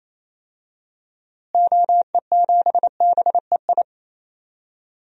Morse code, heard as a keyed tone, about 27 wpm, 700 Hz. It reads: OE7BEI
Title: DXCC-100 Call Signs - 6 Characters - Repeated 2x - 28wpm